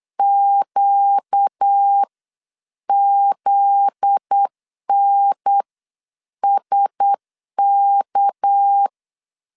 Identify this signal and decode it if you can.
Morse code, heard as a keyed tone, about 8 wpm, spaced as for 7 wpm, 800 Hz. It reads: QZNSK
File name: LEZIONI / 7 GF9Z / 7 GF9Z 8wpm 3213